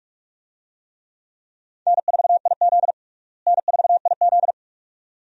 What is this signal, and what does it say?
Morse code, heard as a keyed tone, about 47 wpm, 700 Hz. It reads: N4IZ N4IZ
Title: USA Call Signs - 4 Characters - Repeated 2x - 45wpm